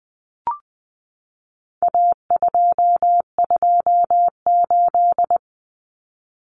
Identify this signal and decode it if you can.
Morse code, heard as a keyed tone, about 20 wpm, 700 Hz. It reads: A228
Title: Common Cut Numbers - 4-Digit 20wpm